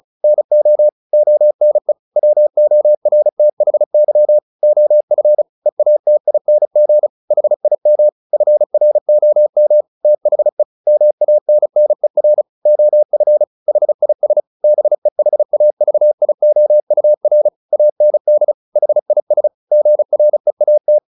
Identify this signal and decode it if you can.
Morse code, heard as a keyed tone, about 35 wpm, 600 Hz. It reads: NO ONE WORTHY OF EATING HIM FROM THE MANNER OF HIS BEHAVIOUR AND HIS GREAT
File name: the old man and the sea 2 35wpm